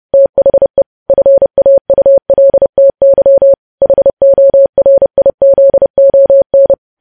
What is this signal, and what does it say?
Morse code, heard as a keyed tone, about 30 wpm, 550 Hz. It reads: THE FAULTY HORIZON
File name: podcast / News headlines-1x-30x30